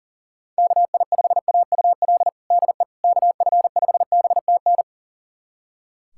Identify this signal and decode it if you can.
Morse code, heard as a keyed tone, about 37 wpm, 700 Hz. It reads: KI5AUL DE KF5BTN
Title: USA Callsign DE Callsign - 6 Characters - 40wpm